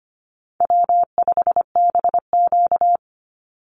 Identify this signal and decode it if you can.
Morse code, heard as a keyed tone, about 25 wpm, 700 Hz. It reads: W5BQ
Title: Easy Call Signs - 25wpm